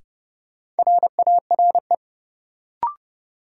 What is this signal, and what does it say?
Morse code, heard as a keyed tone, about 30 wpm, 700 Hz. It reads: RARE
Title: Sentences from Top 1100 Words - New Word Drill - 30wpm